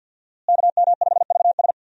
Morse code, heard as a keyed tone, about 47 wpm, 700 Hz. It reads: KM5VS
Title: USA Call Signs - 5 Characters - Repeated 3x - 50wpm